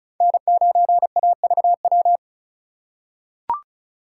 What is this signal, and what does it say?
Morse code, heard as a keyed tone, about 36 wpm, 700 Hz. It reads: N9AVW
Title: USA Call Signs - 5 Characters - 2x Character Spacing - 35wpm